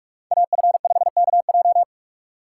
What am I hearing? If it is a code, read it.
Morse code, heard as a keyed tone, about 46 wpm, 700 Hz. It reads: AF5KJ
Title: USA Call Signs - 5 Characters - 4x Character Spacing - 45wpm